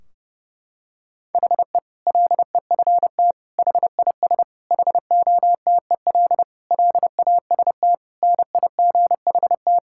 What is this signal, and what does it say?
Morse code, heard as a keyed tone, about 30 wpm, 700 Hz. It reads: HE LEFT HIS HOTEL LAST NIGHT
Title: Sentences from Top 1600 Words 30wpm